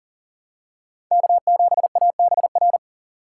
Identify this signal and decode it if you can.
Morse code, heard as a keyed tone, about 40 wpm, 700 Hz. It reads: K7ABR
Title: USA Call Signs - 5 Characters - 2x Character Spacing - 40wpm